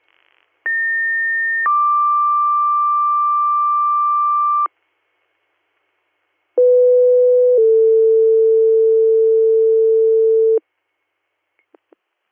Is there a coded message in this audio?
Yes: two-tone page from 0.7 to 10.6 s, heard as a one-second tone, then a three-second tone, about 1800 then 1180 Hz, about 500 then 450 Hz.